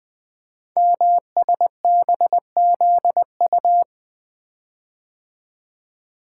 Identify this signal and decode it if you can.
Morse code, heard as a keyed tone, about 20 wpm, 700 Hz. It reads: MSBZU
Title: Code Groups - 5 Letters - 20wpm